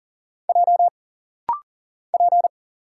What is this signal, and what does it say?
Morse code, heard as a keyed tone, about 40 wpm, 700 Hz. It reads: J P
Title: Single Letter-Number - Rapid-Fire 40wpm